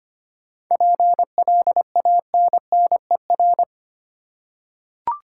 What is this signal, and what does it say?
Morse code, heard as a keyed tone, about 25 wpm, 700 Hz. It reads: PLANNER
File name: practice / 25wpm / Difficult Words with Hints - 25wpm